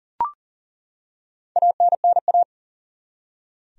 Morse code, heard as a keyed tone, about 40 wpm, 700 Hz. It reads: ANNA
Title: USA Common Names - Max 5 Letters - 40wpm